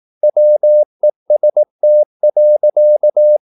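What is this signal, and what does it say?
Morse code, heard as a keyed tone, about 18 wpm, 600 Hz. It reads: WEST.
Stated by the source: Morse practice at 18 wpm, English text, from CW med Gnister.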